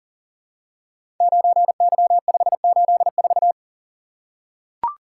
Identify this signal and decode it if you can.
Morse code, heard as a keyed tone, about 42 wpm, 700 Hz. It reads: 9Y584
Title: Code Groups - 5 Letter:Numbers - 40wpm